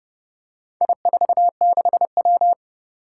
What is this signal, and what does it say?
Morse code, heard as a keyed tone, about 30 wpm, 700 Hz. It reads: I46W
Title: Four Challenging Letter-Number Combinations 30wpm